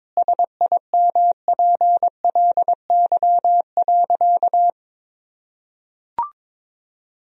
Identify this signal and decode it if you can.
Morse code, heard as a keyed tone, about 22 wpm, 700 Hz. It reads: SIMPLY.
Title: Sentences from Top 1000 Words - Review Part 4 - 22wpm